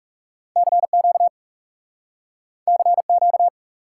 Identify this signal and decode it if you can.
Morse code, heard as a keyed tone, about 40 wpm, 700 Hz. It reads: CQ CQ